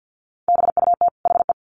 Morse code, heard as a keyed tone, about 50 wpm, 700 Hz. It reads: BUT HE
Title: Sets of 2 Words 50wpm